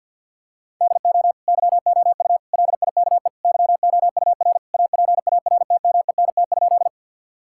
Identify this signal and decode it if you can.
Morse code, heard as a keyed tone, about 50 wpm, 700 Hz. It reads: DO YOU LIKE YOUR APARTMENT?
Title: Sentences from Top 1000 Words - Review Part 6 - 50wpm